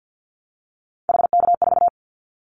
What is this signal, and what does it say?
Morse code, heard as a keyed tone, about 48 wpm, 700 Hz. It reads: HK4
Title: Callsign Prefixes for Top 100 DXCC - 50wpm